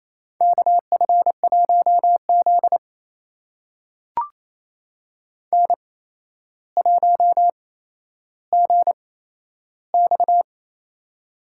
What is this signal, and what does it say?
Morse code, heard as a keyed tone, about 28 wpm, spaced as for 12 wpm, 700 Hz. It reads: KF1Z N1GX
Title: USA Call Signs - 4 Characters - 8x Character Spacing - 28wpm